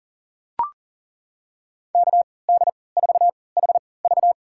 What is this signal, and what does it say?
Morse code, heard as a keyed tone, about 40 wpm, spaced as for 23 wpm, 700 Hz. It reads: KD4HV
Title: USA Call Signs - 5 Characters - 3x Character Spacing - 40wpm